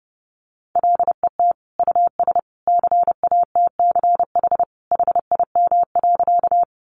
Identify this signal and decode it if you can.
Morse code, heard as a keyed tone, about 30 wpm, 700 Hz. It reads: LET US CATCH HIM.
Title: Sentences from Top 1500 Words 30wpm